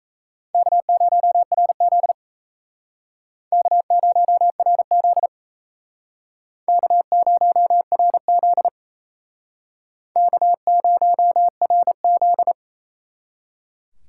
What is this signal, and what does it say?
Morse code, heard as a keyed tone, about 38 wpm, 700 Hz. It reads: K0RZ K0RZ K0RZ K0RZ